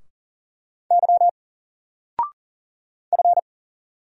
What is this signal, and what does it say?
Morse code, heard as a keyed tone, about 39 wpm, 700 Hz. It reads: Y F
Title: Single Character (Letter:Number:Punctuation:Prosign) 40wpm